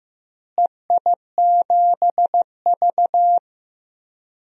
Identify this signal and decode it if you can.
Morse code, heard as a keyed tone, about 15 wpm, 700 Hz. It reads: EI7V
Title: DXCC-100 Call Signs - 4 Characters - Repeated 3x - 15wpm